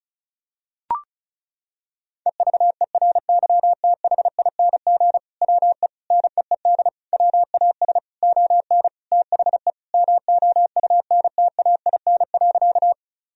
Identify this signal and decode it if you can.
Morse code, heard as a keyed tone, about 36 wpm, 700 Hz. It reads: EVERYTHING WE NEED WAS ON THE MOUNTAIN.
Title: Sentences from Top 1000 Words - Review Part 1 - 35wpm